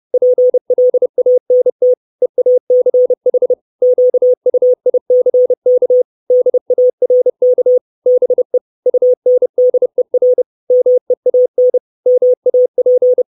Morse code, heard as a keyed tone, about 30 wpm, 500 Hz. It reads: PLANT EACH QUICK DARK BE UNDER MEAN MAP
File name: 632 Words 30WPM